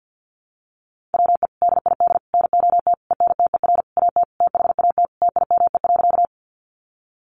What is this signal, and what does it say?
Morse code, heard as a keyed tone, about 50 wpm, 700 Hz. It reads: WE DID NOT ENTER AT THAT TIME.
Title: Sentences from Top 1300 Words 50wpm